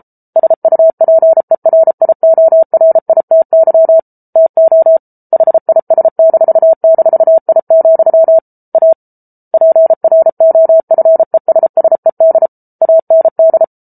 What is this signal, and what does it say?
Morse code, heard as a keyed tone, about 33 wpm, 650 Hz. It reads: SUPERIORITY TO HIS--I, A PROFESSED AND